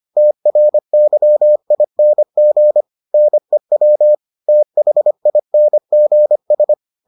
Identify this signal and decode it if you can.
Morse code, heard as a keyed tone, about 25 wpm, 600 Hz. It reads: TRYING NEW THINGS